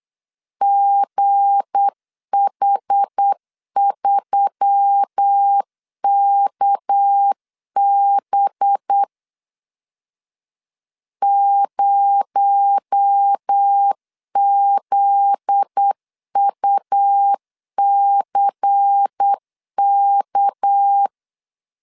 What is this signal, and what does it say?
Morse code, heard as a keyed tone, about 8 wpm, 800 Hz. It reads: GH3KB 0ZUCK
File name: LEZIONI / 7 GF9Z / 7 GF9Z 8wpm 3213